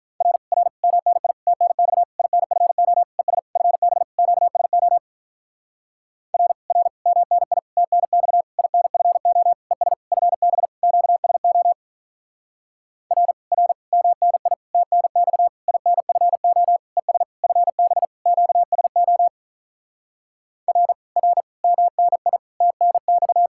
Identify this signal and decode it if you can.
Morse code, heard as a keyed tone, about 47 wpm, 700 Hz. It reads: R R MNI TNX INFO ES FB QSO R R MNI TNX INFO ES FB QSO R R MNI TNX INFO ES FB QSO R R MNI TNX